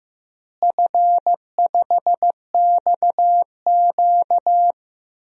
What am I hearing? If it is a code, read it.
Morse code, heard as a keyed tone, about 15 wpm, 700 Hz. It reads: F5XQ
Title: DXCC-100 Call Signs - 4 Characters - Repeated 3x - 15wpm